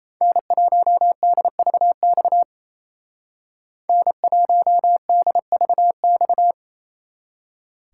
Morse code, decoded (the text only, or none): N1DVX N1DVX